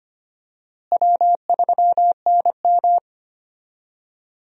Morse code, heard as a keyed tone, about 25 wpm, 700 Hz. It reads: W3NM
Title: USA Call Signs - 4 Characters - Repeated 3x - 25wpm